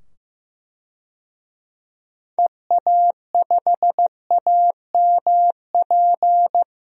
Morse code, heard as a keyed tone, about 15 wpm, 700 Hz. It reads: EA5AMP
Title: DXCC-100 Call Signs - 6 Characters - 15wpm